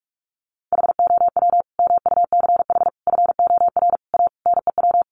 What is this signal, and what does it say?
Morse code, heard as a keyed tone, about 44 wpm, 700 Hz. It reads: HOW MUCH FOR A NEW
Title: Sentences from Top 1500 Words 45wpm